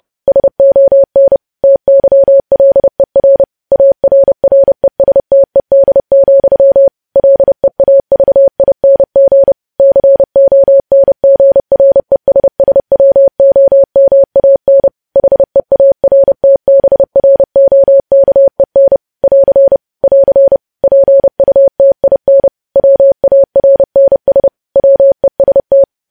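Morse code, heard as a keyed tone, about 30 wpm, 550 Hz. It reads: SON TYLER ARRESTED, LEAVING CONGRESSWOMAN HEARTBROKEN <AR> <AR> PUTIN WARNS WEST
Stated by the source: Morse code sending of news headlines at 30 wpm.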